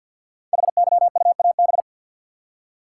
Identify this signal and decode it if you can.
Morse code, heard as a keyed tone, about 51 wpm, 700 Hz. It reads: HYUAB
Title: Code Groups - 5 Letter:Numbers - 50wpm